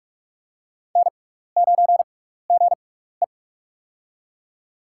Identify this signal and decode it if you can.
Morse code, heard as a keyed tone, about 43 wpm, spaced as for 15 wpm, 700 Hz. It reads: N9GE
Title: USA Call Signs - 4 Characters - 6x Character Spacing - 45wpm